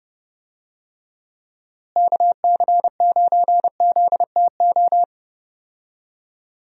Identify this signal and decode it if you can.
Morse code, heard as a keyed tone, about 30 wpm, 700 Hz. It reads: KC9ZTO
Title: USA Call Signs - 6 Characters - Repeated 3x - 30wpm